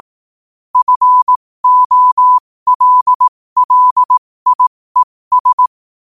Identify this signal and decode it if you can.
Morse code, heard as a keyed tone, about 18 wpm, spaced as for 15 wpm, 1000 Hz. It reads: FOLLIES
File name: qotdfiles / 15wpm